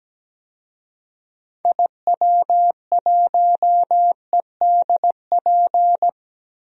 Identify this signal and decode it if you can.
Morse code, heard as a keyed tone, about 17 wpm, 700 Hz. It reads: IW1EDP
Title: DXCC-100 Call Signs - 6 Characters - Repeated 3x - 17wpm